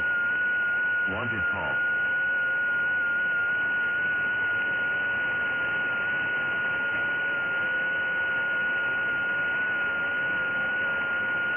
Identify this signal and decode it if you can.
FT8, with speech.
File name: IC-7100